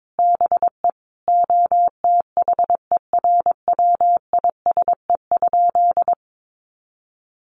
Morse code, heard as a keyed tone, about 22 wpm, 700 Hz. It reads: BE OTHERWISE?
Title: Sentences from Top 1200 Words 22wpm